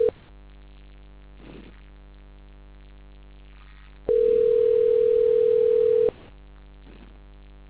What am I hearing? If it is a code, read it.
Ringback tone (North America), 440 + 480 Hz.